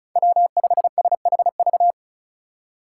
Morse code, heard as a keyed tone, about 34 wpm, 700 Hz. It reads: W5SHV
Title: USA Call Signs - 5 Characters - 4x Character Spacing - 35wpm